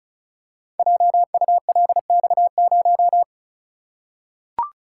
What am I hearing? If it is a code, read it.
Morse code, heard as a keyed tone, about 36 wpm, 690 Hz. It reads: JULX0
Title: Code Groups - 5 Letter:Numbers - 35wpm